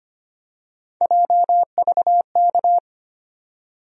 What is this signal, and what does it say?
Morse code, heard as a keyed tone, about 25 wpm, 700 Hz. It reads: JVK
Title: USA Call Sign Postfixes - 3 Letters 25wpm